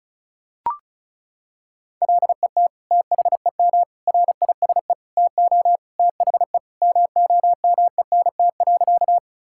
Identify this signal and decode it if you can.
Morse code, heard as a keyed tone, about 36 wpm, 700 Hz. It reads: LET THEM RISE TO THE MOMENT.
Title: Sentences from Top 1000 Words - Review Part 6 - 35wpm